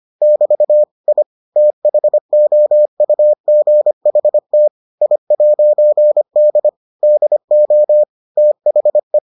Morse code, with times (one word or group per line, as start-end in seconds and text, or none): 0.2-0.8 <BT>
1.1-1.2 I
1.6-4.7 THOUGHT
5.0-6.7 I'D
7.0-8.0 DO
8.4-9.2 THE